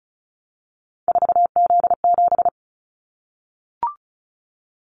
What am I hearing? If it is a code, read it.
Morse code, heard as a keyed tone, about 34 wpm, 700 Hz. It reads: 4Z7
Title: Callsign Prefixes for Top 50 DXCC - 35wpm